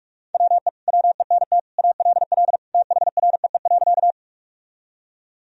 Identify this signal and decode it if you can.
Morse code, heard as a keyed tone, about 46 wpm, 700 Hz. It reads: WE WENT ALL THREE.
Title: Sentences from Top 1000 Words - Review Part 1 - 45wpm